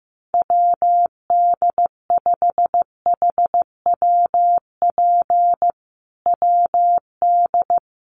Morse code, heard as a keyed tone, about 15 wpm, 700 Hz. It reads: WD5HWP WD
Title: USA Call Signs Repeated 2x 15wpm